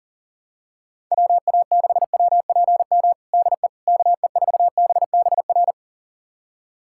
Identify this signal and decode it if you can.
Morse code, heard as a keyed tone, about 40 wpm, 700 Hz. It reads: WA6WPM DE KE4BBR
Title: USA Callsign DE Callsign - 6 Characters - 40wpm